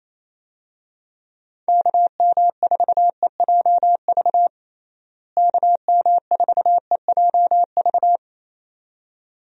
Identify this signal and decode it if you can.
Morse code, heard as a keyed tone, about 28 wpm, 700 Hz. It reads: KM4EJV KM4EJV
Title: USA Call Signs - 6 Characters - Repeated 2x - 28wpm